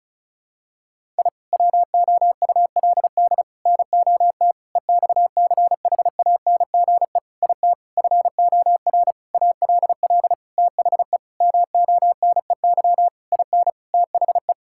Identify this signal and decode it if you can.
Morse code, heard as a keyed tone, about 36 wpm, 700 Hz. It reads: I WOULD NOT EXCHANGE IT FOR ALL THE MONEY IN THE
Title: Sentences from Top 1000 Words - Review Part 6 - 35wpm